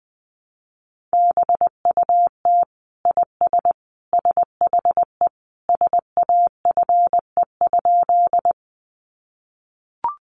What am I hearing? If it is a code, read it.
Morse code, heard as a keyed tone, about 20 wpm, 700 Hz. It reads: BUT IS SHE SAFE?
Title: Sentences from Top 900 Words 20wpm